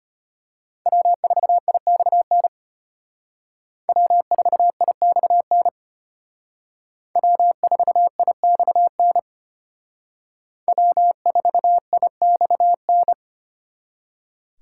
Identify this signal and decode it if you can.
Morse code, heard as a keyed tone, about 32 wpm, 700 Hz. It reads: W4IXN W4IXN W4IXN W4IXN